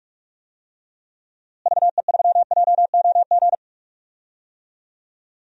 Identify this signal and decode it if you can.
Morse code, heard as a keyed tone, about 41 wpm, 700 Hz. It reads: VE3JOG
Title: DXCC-100 Call Signs - 6 Characters - Repeated 3x - 45wpm